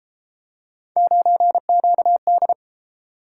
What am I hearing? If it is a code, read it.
Morse code, heard as a keyed tone, about 31 wpm, 700 Hz. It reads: 9QD